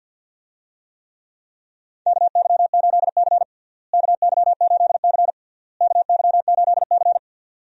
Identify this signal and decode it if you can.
Morse code, heard as a keyed tone, about 51 wpm, 700 Hz. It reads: KY8C KY8C KY8C
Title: USA Call Signs - 4 Characters - Repeated 3x - 50wpm